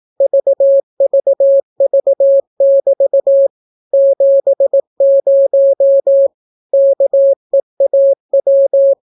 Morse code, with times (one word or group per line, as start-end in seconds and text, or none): 0.2-3.5 VVV<BT>
3.9-6.3 70
6.7-8.9 KEAW